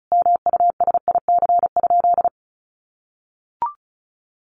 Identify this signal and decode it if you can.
Morse code, heard as a keyed tone, about 34 wpm, 700 Hz. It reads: MUSIC?
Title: Sentences from Top 1000 Words 35wpm